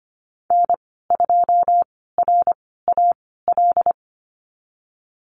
Morse code, heard as a keyed tone, about 25 wpm, spaced as for 16 wpm, 700 Hz. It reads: N2RAL
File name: Lesson 15g - Number 5 - Call Signs 25wpm